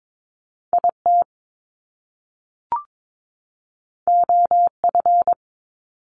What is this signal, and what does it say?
Morse code, heard as a keyed tone, about 22 wpm, 700 Hz. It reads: IT OF
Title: Top Two Letter Combinations in Words 22wpm